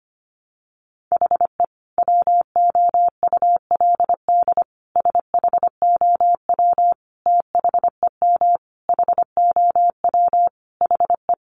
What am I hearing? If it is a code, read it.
Morse code, heard as a keyed tone, about 25 wpm, 700 Hz. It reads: HE WOULD SHOW THEM HOW HE
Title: Sentences from Top 1300 Words 25wpm